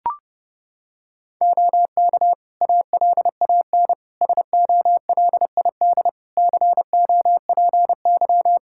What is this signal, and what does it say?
Morse code, heard as a keyed tone, about 30 wpm, 700 Hz. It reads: OK ALAN SOLID COPY
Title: Practice QSO - Exchange 3 - More Info - 30wpm